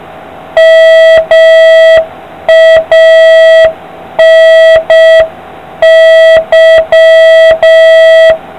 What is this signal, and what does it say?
Morse code, heard as a keyed tone, about 6 wpm, 650 Hz, with noise. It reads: MANY